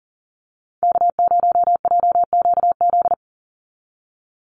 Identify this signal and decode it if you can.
Morse code, heard as a keyed tone, about 40 wpm, 700 Hz. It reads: K0JQZ
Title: NA SOTA Chaser Callsigns 40wpm